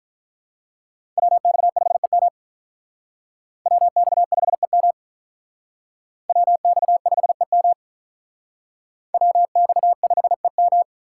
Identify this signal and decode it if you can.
Morse code, heard as a keyed tone, about 43 wpm, 700 Hz. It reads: WX5EM WX5EM WX5EM WX5EM
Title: Speed Racing - USA Call Signs 35wpm